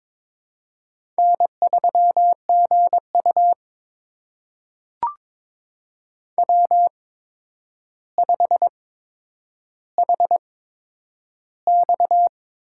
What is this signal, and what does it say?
Morse code, heard as a keyed tone, about 21 wpm, spaced as for 9 wpm, 700 Hz. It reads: N3GU W5HX
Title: USA Call Signs - 4 Characters - 8x Character Spacing - 22wpm